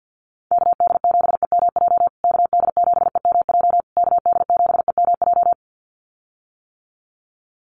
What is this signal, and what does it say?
Morse code, heard as a keyed tone, about 50 wpm, 700 Hz. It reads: KD7EMJ KD7EMJ KD7EMJ
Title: USA Call Signs Repeated 3x 50wpm